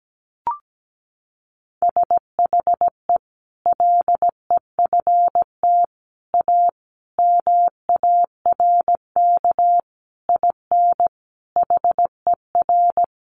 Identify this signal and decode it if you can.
Morse code, heard as a keyed tone, about 17 wpm, 700 Hz. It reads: SHE LEFT A MARK IN HER
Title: Sentences from Top 1100 Words 17wpm